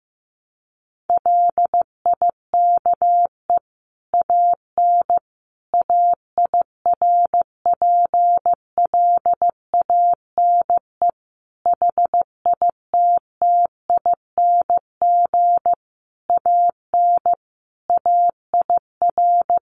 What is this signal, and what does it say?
Morse code, heard as a keyed tone, about 15 wpm, 700 Hz. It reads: LIKE AN AIRPLANE HITTING AN AIR
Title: A Sound Of Thunder 15wpm